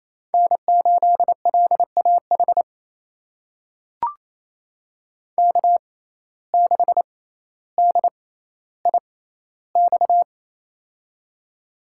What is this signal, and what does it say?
Morse code, heard as a keyed tone, about 28 wpm, spaced as for 14 wpm, 700 Hz. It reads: N8LAH K6DIX
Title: USA Call Signs - 5 Characters - 6x Character Spacing - 28wpm